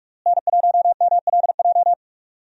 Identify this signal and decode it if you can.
Morse code, heard as a keyed tone, about 46 wpm, 700 Hz. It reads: N1MLJ
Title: USA Call Signs - 5 Characters - Repeated 3x - 45wpm